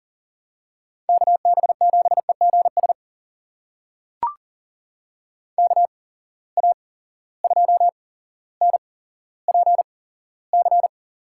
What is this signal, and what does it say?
Morse code, heard as a keyed tone, about 40 wpm, spaced as for 17 wpm, 700 Hz. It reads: KB7EGS KA2NPC
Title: USA Call Signs - 6 Characters - 8x Character Spacing - 40wpm